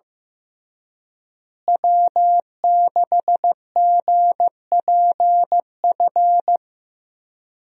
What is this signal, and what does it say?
Morse code, heard as a keyed tone, about 14 wpm, 700 Hz. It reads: W6GPF
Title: USA Call Signs - 5 Characters - Repeated 3x - 15wpm